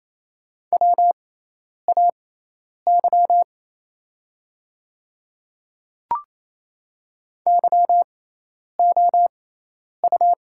Morse code, heard as a keyed tone, about 28 wpm, spaced as for 10 wpm, 700 Hz. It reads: WAY YOU
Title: 6x Character Spacing - Top 300 Words - 28wpm